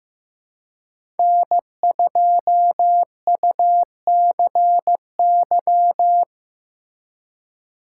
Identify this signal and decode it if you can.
Morse code, heard as a keyed tone, about 15 wpm, 700 Hz. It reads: N2UCY
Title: USA Call Signs - 5 Characters - 2x Character Spacing - 15wpm